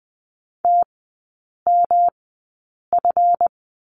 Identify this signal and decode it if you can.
Morse code, heard as a keyed tone, about 20 wpm, spaced as for 8 wpm, 700 Hz. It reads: TMF
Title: 138 Lesson 23e - Letter f - Three Characters 20wpm